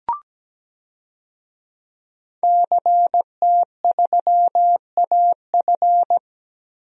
Morse code, heard as a keyed tone, about 17 wpm, 700 Hz. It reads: CT3AF
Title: DXCC-100 Call Signs - 5 Characters - Repeated 2x - 17wpm